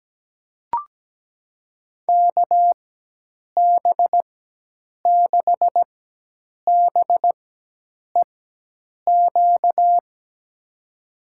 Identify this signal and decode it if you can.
Morse code, heard as a keyed tone, about 17 wpm, 700 Hz. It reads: K B 6 B E Q
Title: USA Call Signs - 6 Characters - 4x Character Spacing - 17wpm